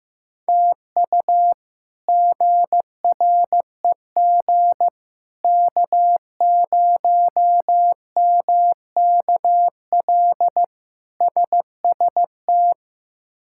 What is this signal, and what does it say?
Morse code, heard as a keyed tone, about 15 wpm, 700 Hz. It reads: TU GREG K0MKL SST
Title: K1USN SST QSO Practice - 15wpm